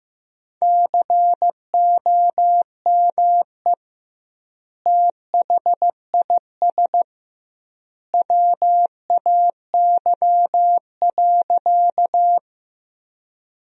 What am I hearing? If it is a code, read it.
Morse code, heard as a keyed tone, about 16 wpm, 700 Hz. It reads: COME THIS WAY.
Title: 2x Word-Spacing - Sentences from Top 100 Words - 15wpm